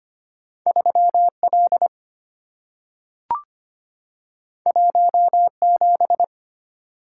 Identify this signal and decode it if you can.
Morse code, heard as a keyed tone, about 25 wpm, 690 Hz. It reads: 3L 17
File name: Two Letter-Number Combinations - 25wpm